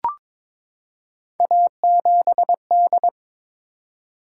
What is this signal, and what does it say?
Morse code, heard as a keyed tone, about 22 wpm, 700 Hz. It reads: A7D
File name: practice / 22wpm / Three Letter-Number Combinations - 22wpm